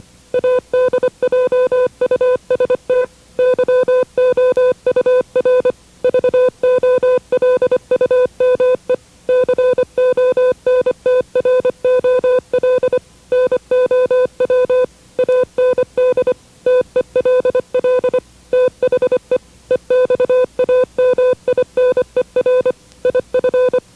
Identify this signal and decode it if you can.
Morse code, heard as a keyed tone, about 25 wpm, 500 Hz. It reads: ADJUST YOUR VOLUME CONTROL NOW AND TELL THE EXAMINER IF
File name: WRTC1990-CW-Pileup